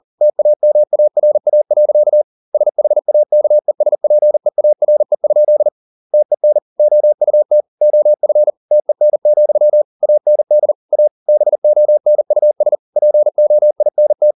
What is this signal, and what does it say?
Morse code, heard as a keyed tone, about 40 wpm, 600 Hz. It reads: TAMARA. SHAKESPEARE? TEN OUT OF TEN, AND A BONUS POINT